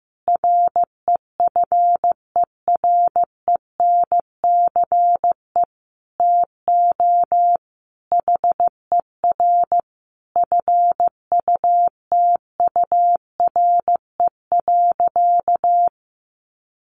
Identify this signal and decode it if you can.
Morse code, heard as a keyed tone, about 15 wpm, 700 Hz. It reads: REFERENCE TO HER FUTURE.
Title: Sentences from Top 1200 Words 15wpm